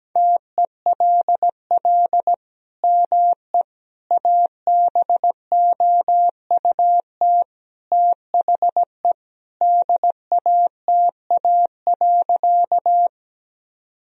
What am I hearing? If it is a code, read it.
Morse code, heard as a keyed tone, about 17 wpm, 700 Hz. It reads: TELL ME ABOUT THE DATA.
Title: Sentences from Top 1000 Words - Review Part 3 - 17wpm